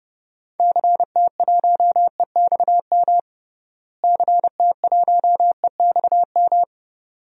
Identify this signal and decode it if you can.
Morse code, heard as a keyed tone, about 32 wpm, 700 Hz. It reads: CT1EXM CT1EXM
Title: DXCC-100 Call Signs - 6 Characters - Repeated 2x - 30wpm